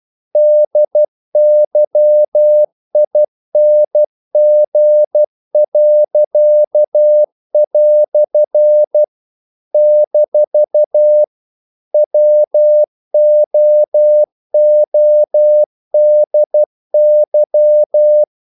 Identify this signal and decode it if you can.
Morse code, heard as a keyed tone, about 12 wpm, 600 Hz. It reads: DYING." - WOODY
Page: English quotations in CW (telegraphy - morse)